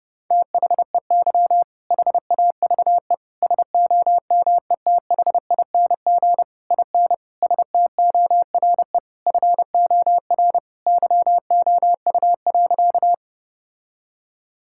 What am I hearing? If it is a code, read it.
Morse code, heard as a keyed tone, about 32 wpm, 700 Hz. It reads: THEY HAVE SOMETHING IN STORE FOR YOU.
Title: Sentences from Top 1000 Words - Review Part 5 - 30wpm